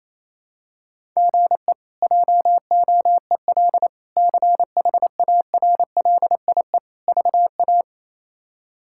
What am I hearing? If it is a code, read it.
Morse code, heard as a keyed tone, about 28 wpm, 700 Hz. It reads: GE JOEL CHARLIE VA
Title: K1USN SST QSO Practice - 28wpm